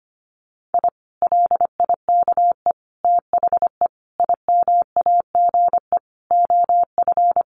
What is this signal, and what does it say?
Morse code, heard as a keyed tone, about 25 wpm, 700 Hz. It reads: I LIKE THE IMAGE OF
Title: Sentences from Top 700 Words 25wpm